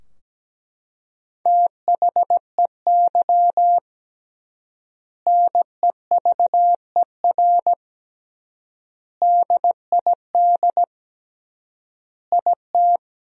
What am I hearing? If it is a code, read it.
Morse code, heard as a keyed tone, about 17 wpm, 700 Hz. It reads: THEY NEVER DID IT
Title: 3x Word-Spacing - Sentences from Top 300 Words - 17wpm